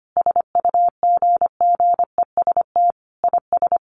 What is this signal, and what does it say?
Morse code, heard as a keyed tone, about 25 wpm, 700 Hz. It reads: SUGGEST IS
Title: Sentences from Top 1300 Words 25wpm